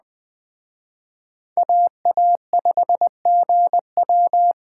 Morse code, heard as a keyed tone, about 25 wpm, 700 Hz. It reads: AA5GW
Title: Speed Racing - USA Call Signs 20wpm